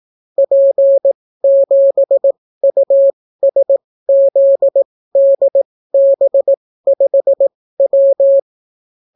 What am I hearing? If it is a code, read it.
Morse code, heard as a keyed tone, about 18 wpm, spaced as for 15 wpm, 550 Hz. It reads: P7USZDB5W